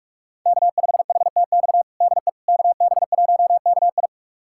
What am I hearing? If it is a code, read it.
Morse code, heard as a keyed tone, about 46 wpm, 700 Hz. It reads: K5HTX DE KB1KI
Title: USA Callsign DE Callsign - 5 Characters - 45wpm